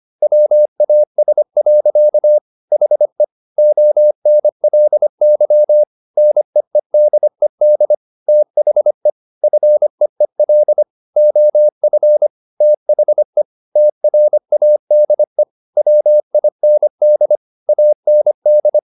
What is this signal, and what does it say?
Morse code, heard as a keyed tone, about 25 wpm, 600 Hz. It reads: WAS. HE ONLY NEEDED THE FEEL OF THE TRADE WIND AND